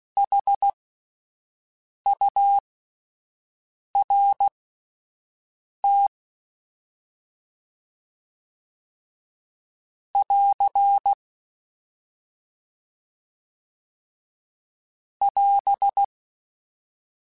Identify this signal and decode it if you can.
Morse code, heard as a keyed tone, about 16 wpm, spaced as for 5 wpm, 800 Hz. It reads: HURT <AR> <AS>
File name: S vs H / S vs H mp3s / S vs H @5